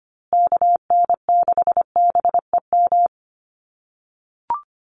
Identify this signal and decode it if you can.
Morse code, heard as a keyed tone, about 25 wpm, 700 Hz. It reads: KN6BEM
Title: USA Call Signs Repeated 3x 25wpm